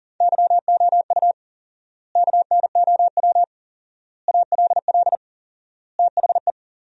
Morse code, heard as a keyed tone, about 40 wpm, 700 Hz. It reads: YOU KNOW ALL THE
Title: 4x Word-Spacing - Sentences from Top 300 Words - 40wpm